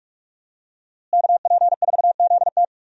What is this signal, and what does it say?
Morse code, heard as a keyed tone, about 42 wpm, 700 Hz. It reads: KP4ZT
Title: USA Call Signs - 5 Characters - Repeated 2x - 45wpm